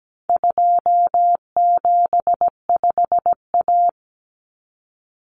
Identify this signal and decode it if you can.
Morse code, heard as a keyed tone, about 17 wpm, 700 Hz. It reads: 275A
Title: Common Cut Numbers - 4-Digit 17wpm